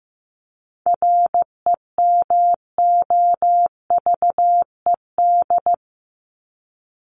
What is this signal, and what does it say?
Morse code, heard as a keyed tone, about 15 wpm, 700 Hz. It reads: REMOVED
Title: ED Suffix - Top 100 Words with Base Words 15wpm